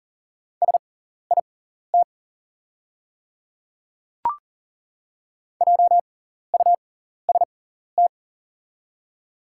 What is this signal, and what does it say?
Morse code, heard as a keyed tone, about 40 wpm, spaced as for 14 wpm, 700 Hz. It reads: SIT JUST
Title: 6x Character Spacing - Top 500 Words - 40wpm